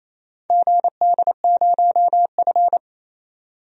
Morse code, heard as a keyed tone, about 28 wpm, 700 Hz. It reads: GD0F
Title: DXCC-100 Call Signs - 4 Characters - 28wpm